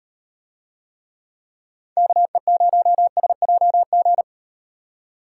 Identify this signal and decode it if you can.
Morse code, heard as a keyed tone, about 35 wpm, 700 Hz. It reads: KE0SJG